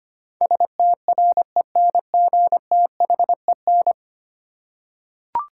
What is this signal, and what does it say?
Morse code, heard as a keyed tone, about 25 wpm, 700 Hz. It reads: STRENGTHEN
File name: Ten Letter Words - 25wpm